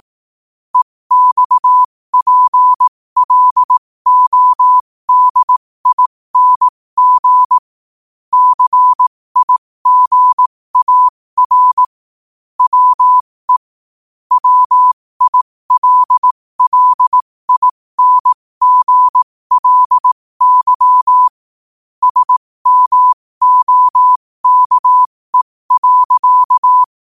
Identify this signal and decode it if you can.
Morse code, heard as a keyed tone, about 18 wpm, spaced as for 15 wpm, 1000 Hz. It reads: EXPLODING CIGAR WE WILLINGLY SMOKE.
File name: qotdfiles / 15wpm